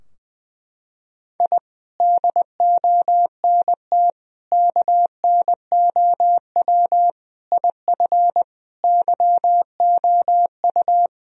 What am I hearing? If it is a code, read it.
Morse code, heard as a keyed tone, about 20 wpm, 700 Hz. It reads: I DONT KNOW IF YOU
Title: Sentences from Top 1000 Words - Review Part 2 - 20wpm